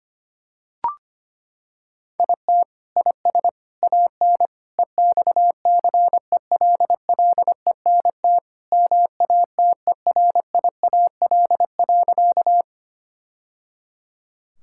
Morse code, heard as a keyed tone, about 25 wpm, 700 Hz. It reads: IT IS AN EXCELLENT MATERIAL.
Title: Sentences from Top 1000 Words - Review Part 5 - 25wpm